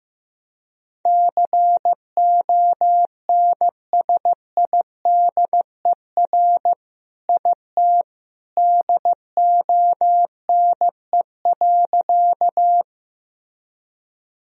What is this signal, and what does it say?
Morse code, heard as a keyed tone, about 15 wpm, 700 Hz. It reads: CONSIDER IT DONE.
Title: Sentences from Top 1000 Words - Review Part 4 - 15wpm